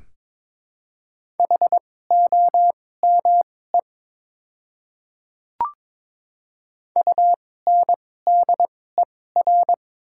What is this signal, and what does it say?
Morse code, heard as a keyed tone, about 22 wpm, spaced as for 16 wpm, 700 Hz. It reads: HOME UNDER
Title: 2x Character Spacing - Top 200 Words - 22wpm